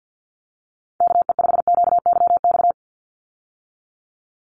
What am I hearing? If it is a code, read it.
Morse code, heard as a keyed tone, about 50 wpm, 700 Hz. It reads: KE5QYX
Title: USA Call Signs 50wpm